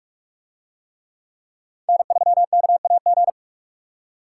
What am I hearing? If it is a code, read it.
Morse code, heard as a keyed tone, about 44 wpm, 700 Hz. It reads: N3KAG